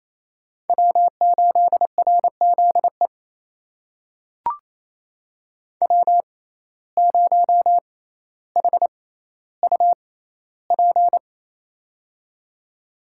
Morse code, heard as a keyed tone, about 28 wpm, spaced as for 14 wpm, 700 Hz. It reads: W8RZE W0HUP
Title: USA Call Signs - 5 Characters - 6x Character Spacing - 28wpm